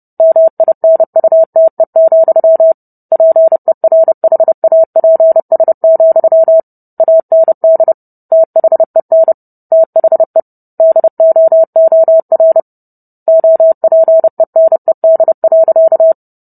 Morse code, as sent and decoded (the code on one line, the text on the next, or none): -- .. -. ..- - . --..-- / .--. .-. .... .- .--. ... --..-- / .- -. -.. / - .... -. / - .... / -.. --- --- .-. / --- .--. -. -.. .-.-.-
MINUTE, PERHAPS, AND THEN THE DOOR OPENED.